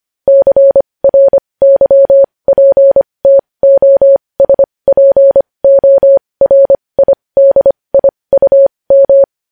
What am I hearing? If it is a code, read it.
Morse code, heard as a keyed tone, about 25 wpm, spaced as for 20 wpm, 550 Hz. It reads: CRYPTOSPORIDIUM
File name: podcast / News headlines-1x-25x20